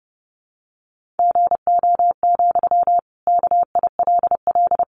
Morse code, heard as a keyed tone, about 30 wpm, 700 Hz. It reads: GO, KILL
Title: Sentences from Top 1400 Words 30wpm